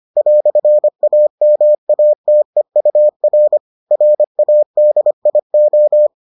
Morse code, heard as a keyed tone, about 25 wpm, 600 Hz. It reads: "AMATEUR RADIO